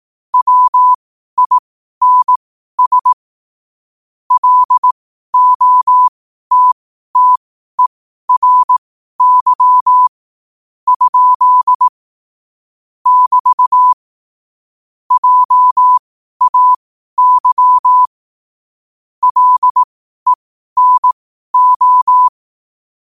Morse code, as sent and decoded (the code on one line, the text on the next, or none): .-- .. -. ... / .-.. --- - - . .-. -.-- / ..--.. / -...- / .--- .- -.-- / .-.. -. ---
WINS LOTTERY ? <BT> JAY LENO